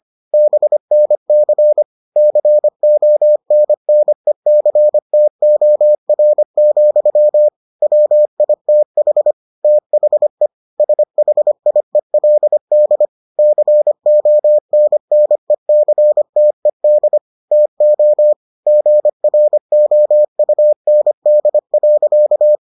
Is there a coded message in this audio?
Yes: Morse code from 0.3 to 22.6 s, heard as a keyed tone, about 25 wpm, 600 Hz. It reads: BNC CONNECTOR, WITH THE SHIELD CONNECTED TO GROUND.